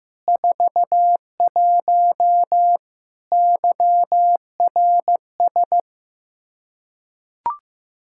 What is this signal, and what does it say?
Morse code, heard as a keyed tone, about 15 wpm, 700 Hz. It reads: 41 YRS
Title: Practice QSO - Exchange 3 - More Info - 15wpm